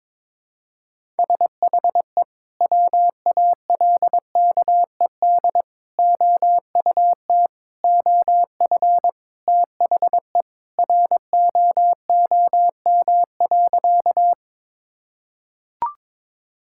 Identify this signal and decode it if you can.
Morse code, heard as a keyed tone, about 22 wpm, 700 Hz. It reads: SHE WALKED OUT OF THE ROOM.
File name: practice / 22wpm / Sentences with 3-Word Phrases - 22wpm